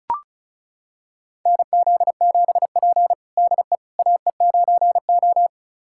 Morse code, heard as a keyed tone, about 36 wpm, 700 Hz. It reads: NZ7P DE AE9O
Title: USA Callsign DE Callsign - 4 Characters - 35wpm